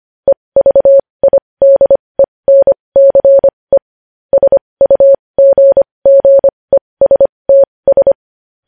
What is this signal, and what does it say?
Morse code, heard as a keyed tone, about 25 wpm, spaced as for 20 wpm, 550 Hz. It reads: EVIDENCE SUGGESTS
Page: News Headlines in Morse Code at 20 WPM